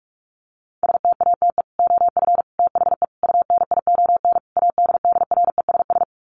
Morse code, heard as a keyed tone, about 44 wpm, 700 Hz. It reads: STATE OF THE UNION ADDRESS
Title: Sets of 5 Words Encore 45wpm